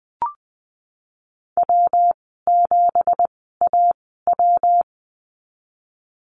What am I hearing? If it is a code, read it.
Morse code, heard as a keyed tone, about 20 wpm, spaced as for 15 wpm, 700 Hz. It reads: W7AW
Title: 200 Lesson 32g - Letter k - Call Signs 20wpm